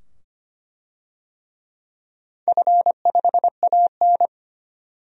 Morse code, heard as a keyed tone, about 25 wpm, 700 Hz. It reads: F5AN
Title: DXCC-100 Call Signs - 4 Characters - 25wpm